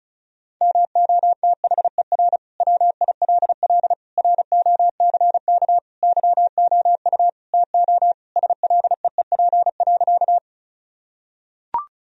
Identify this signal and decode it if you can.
Morse code, heard as a keyed tone, about 36 wpm, 700 Hz. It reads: MOTHER WILL ROCK YOU TO SLEEP.
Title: Sentences from Top 1000 Words - Review Part 2 - 35wpm